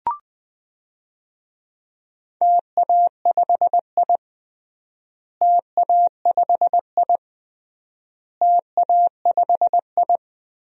Morse code, heard as a keyed tone, about 20 wpm, 700 Hz. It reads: TA5I TA5I TA5I
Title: DXCC-100 Call Signs - 4 Characters - Repeated 3x - 20wpm